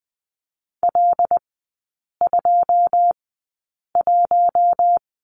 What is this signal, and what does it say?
Morse code, heard as a keyed tone, about 20 wpm, spaced as for 8 wpm, 690 Hz. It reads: L21
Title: 081 Lesson 14e - Number 2 - Three Characters 20wpm